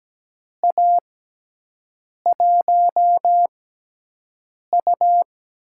Morse code, heard as a keyed tone, about 17 wpm, spaced as for 6 wpm, 700 Hz. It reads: A1U
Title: USA Call Signs - 6x Character Spacing - 17wpm